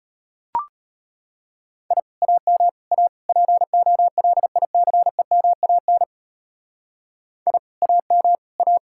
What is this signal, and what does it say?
Morse code, heard as a keyed tone, about 32 wpm, 700 Hz. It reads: I AM A POLICEMAN I AM A